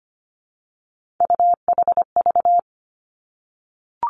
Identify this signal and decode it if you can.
Morse code, heard as a keyed tone, about 25 wpm, 700 Hz. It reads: UHV
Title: Three Challenging Letter-Number Combinations 25wpm